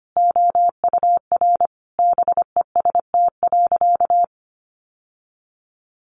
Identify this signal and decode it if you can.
Morse code, heard as a keyed tone, about 25 wpm, 700 Hz. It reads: OUR BEST.
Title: Sentences from Top 800 Words 25wpm